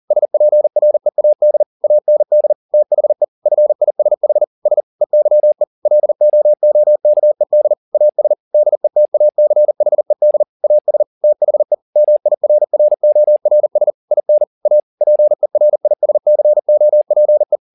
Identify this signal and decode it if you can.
Morse code, heard as a keyed tone, about 40 wpm, 600 Hz. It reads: SPREAD AND THE FISH S EYE LOOKED AS DETACHED AS THE MIRRORS IN A PERISCOPE
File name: the old man and the sea 2 40wpm